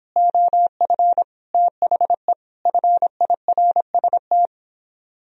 Morse code, heard as a keyed tone, about 31 wpm, 700 Hz. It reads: OF THE FIRST